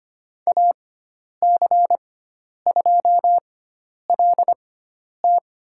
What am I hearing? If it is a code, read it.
Morse code, heard as a keyed tone, about 25 wpm, spaced as for 10 wpm, 700 Hz. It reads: AC2LT
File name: course / 25:10wpm / 095 Lesson 16g - Letter c - Call Signs 25wpm